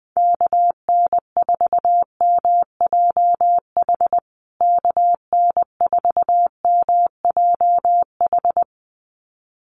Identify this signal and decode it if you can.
Morse code, heard as a keyed tone, about 20 wpm, 700 Hz. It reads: KN4MJH KN4MJH
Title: USA Call Signs Repeated 2x 20wpm